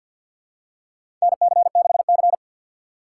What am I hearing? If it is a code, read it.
Morse code, heard as a keyed tone, about 53 wpm, 690 Hz. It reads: NC6C
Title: USA Call Signs - 6x Character Spacing - 50wpm